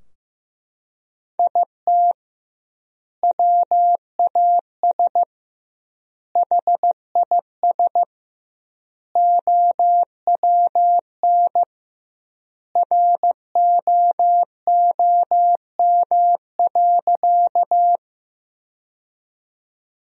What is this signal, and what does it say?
Morse code, heard as a keyed tone, about 15 wpm, 700 Hz. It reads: IT WAS HIS OWN ROOM.
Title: 2x Word-Spacing - Sentences from Top 300 Words - 15wpm